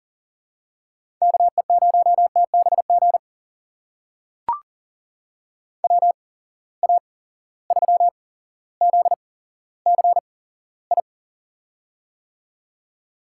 Morse code, heard as a keyed tone, about 40 wpm, spaced as for 17 wpm, 700 Hz. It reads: KE0TBG WA3ZCI
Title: USA Call Signs - 6 Characters - 8x Character Spacing - 40wpm